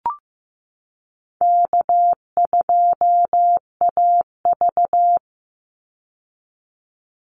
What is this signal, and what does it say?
Morse code, heard as a keyed tone, about 15 wpm, 700 Hz. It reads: K2AV
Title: Easy Call Signs Repeated 3x - 15wpm